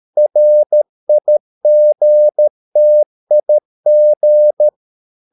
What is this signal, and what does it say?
Morse code, heard as a keyed tone, about 13 wpm, 600 Hz. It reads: RIGTIG
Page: Ferie 13 wpm